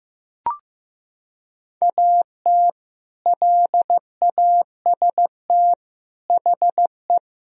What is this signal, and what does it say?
Morse code, heard as a keyed tone, about 15 wpm, 700 Hz. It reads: AT LAST HE
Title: Sentences from Top 1000 Words - Review Part 1 - 15wpm